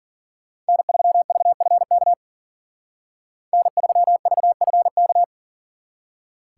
Morse code, heard as a keyed tone, about 52 wpm, 700 Hz. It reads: N3VFK N3VFK